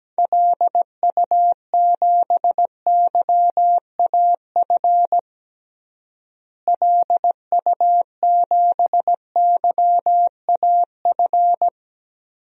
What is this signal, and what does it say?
Morse code, heard as a keyed tone, about 17 wpm, 700 Hz. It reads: LU7YAF LU7YAF
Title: DXCC-100 Call Signs - 6 Characters - Repeated 3x - 17wpm